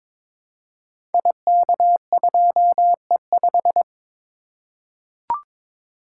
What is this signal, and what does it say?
Morse code, heard as a keyed tone, about 22 wpm, 700 Hz. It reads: IK2E5
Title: Code Groups - 5 Letter:Numbers - 22wpm